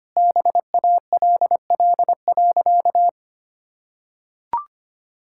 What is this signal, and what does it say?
Morse code, heard as a keyed tone, about 25 wpm, 700 Hz. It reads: BALL.
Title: Sentences from Top 1000 Words - Review Part 6 - 25wpm